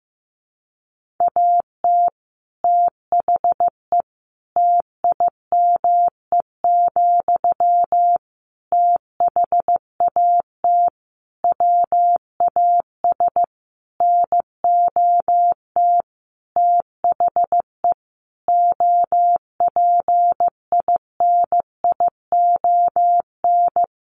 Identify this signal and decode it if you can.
Morse code, heard as a keyed tone, about 15 wpm, 700 Hz. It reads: AT THE TIME, THAT WAS NOT THE OPINION
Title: Sentences from Top 1400 Words 15wpm